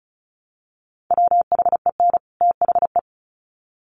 Morse code, heard as a keyed tone, about 34 wpm, 700 Hz. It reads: WHEN THE
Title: Sets of 2 Words 35wpm